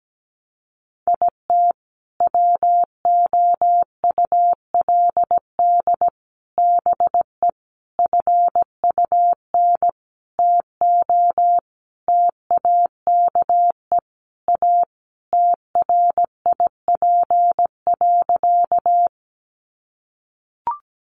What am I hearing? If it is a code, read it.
Morse code, heard as a keyed tone, about 17 wpm, 700 Hz. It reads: IT WOULD BE FUN TO TAKE A TRIP.
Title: Sentences from Top 1300 Words 17wpm